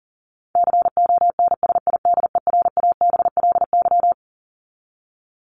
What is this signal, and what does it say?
Morse code, heard as a keyed tone, about 40 wpm, 700 Hz. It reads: CONSIDERABLY
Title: LY Suffix - Top 100 Words with Base Words 40wpm